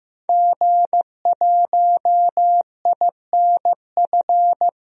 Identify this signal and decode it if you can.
Morse code, heard as a keyed tone, about 15 wpm, 700 Hz. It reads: G1INF
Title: DXCC-100 Call Signs - 5 Characters - 15wpm